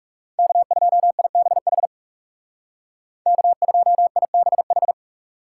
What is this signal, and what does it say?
Morse code, heard as a keyed tone, about 38 wpm, 700 Hz. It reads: K2IBH K2IBH